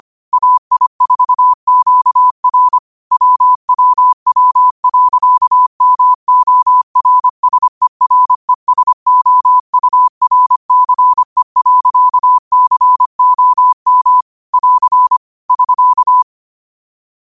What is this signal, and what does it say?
Morse code, heard as a keyed tone, about 25 wpm, 1000 Hz. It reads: AI4QR WWW.MORSERESOURCE.COM <AR> <SK>
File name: qotdfiles / 25wpm